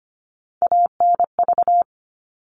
Morse code, heard as a keyed tone, about 25 wpm, 700 Hz. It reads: ANV
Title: All Cut Numbers - 3-Digit 25wpm